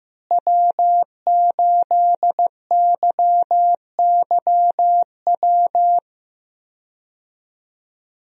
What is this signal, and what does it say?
Morse code, heard as a keyed tone, about 15 wpm, 700 Hz. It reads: W8YYW
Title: USA Call Signs - 5 Characters - Repeated 3x - 15wpm